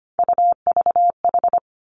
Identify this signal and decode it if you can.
Morse code, heard as a keyed tone, about 25 wpm, 700 Hz. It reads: UVH
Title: Three Challenging Letter-Number Combinations 25wpm